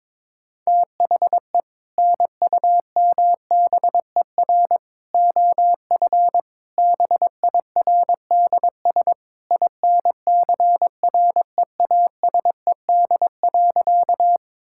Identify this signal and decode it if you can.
Morse code, heard as a keyed tone, about 22 wpm, 700 Hz. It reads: THE NUMBER OF BIRDS INCREASED.
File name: practice / 22wpm / Sentences with 3-Word Phrases - 22wpm